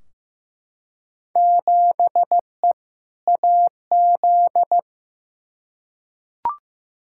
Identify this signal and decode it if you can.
Morse code, heard as a keyed tone, about 16 wpm, 700 Hz. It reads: 7E AZ
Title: Field Day Exchanges Encore - 15wpm